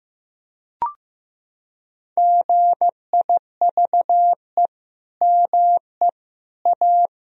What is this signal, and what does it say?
Morse code, heard as a keyed tone, about 15 wpm, 700 Hz. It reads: GIVE ME A
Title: Sentences from Top 1000 Words - Review Part 4 - 15wpm